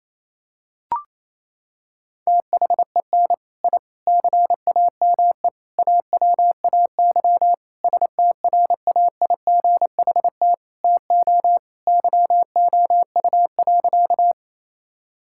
Morse code, heard as a keyed tone, about 28 wpm, 700 Hz. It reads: THEN I CAME AWAY STRAIGHT TO YOU.
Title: Sentences from Top 1000 Words - Review Part 6 - 28wpm